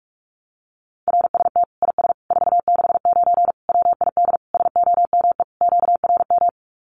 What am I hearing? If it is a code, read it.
Morse code, heard as a keyed tone, about 44 wpm, 700 Hz. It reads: RST IS 469 WID SOME QRM
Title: Common QSO Phrases 45wpm